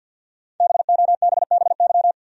Morse code, heard as a keyed tone, about 53 wpm, 700 Hz. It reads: BOBBY
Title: USA Common Names - Max 5 Letters - 50wpm